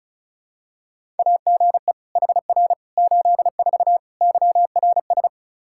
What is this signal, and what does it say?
Morse code, heard as a keyed tone, about 34 wpm, 700 Hz. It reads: AGE HR 84 YRS
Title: Practice QSO - Exchange 3 - More Info - 35wpm